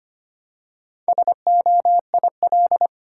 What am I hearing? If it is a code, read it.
Morse code, heard as a keyed tone, about 25 wpm, 700 Hz. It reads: SOIL
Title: Four Letter Words - 25wpm